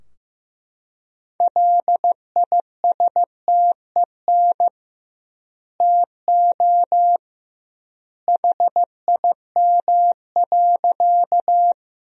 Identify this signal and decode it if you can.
Morse code, heard as a keyed tone, about 16 wpm, 700 Hz. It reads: LISTEN TO HIM.
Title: 2x Word-Spacing - Sentences from Top 400 Words - 15wpm